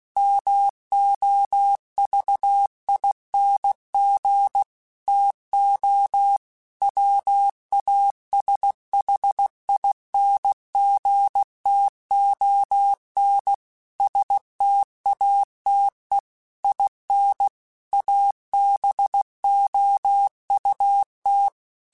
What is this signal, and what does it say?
Morse code, heard as a keyed tone, about 16 wpm, 800 Hz. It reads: MOVING TO WASHINGTON STATE IN ABOUT
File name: QSOs / QSOs-mp3s / WD0BGP 1@15